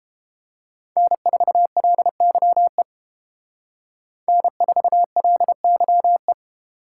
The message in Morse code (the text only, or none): N4LYE N4LYE